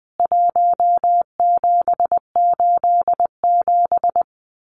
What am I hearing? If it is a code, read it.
Morse code, heard as a keyed tone, about 20 wpm, 700 Hz. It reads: 1787
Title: CWT Exchange 20wpm